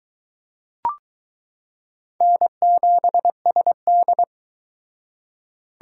Morse code, heard as a keyed tone, about 22 wpm, 700 Hz. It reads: N7SD